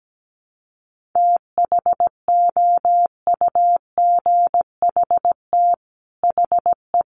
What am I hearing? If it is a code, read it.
Morse code, heard as a keyed tone, about 17 wpm, 700 Hz. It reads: THOUGHT HE
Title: Sets of 5 Words Encore2 17wpm